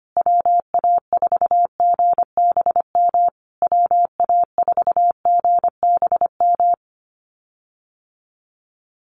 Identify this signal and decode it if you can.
Morse code, heard as a keyed tone, about 25 wpm, 700 Hz. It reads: WA4GBM WA4GBM
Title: USA Call Signs Repeated 3x 25wpm